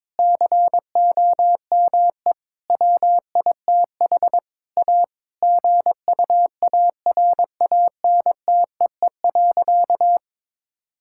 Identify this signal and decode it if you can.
Morse code, heard as a keyed tone, about 22 wpm, 700 Hz. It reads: COME WITH A GUARANTEE.
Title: Sentences from Top 1600 Words 22wpm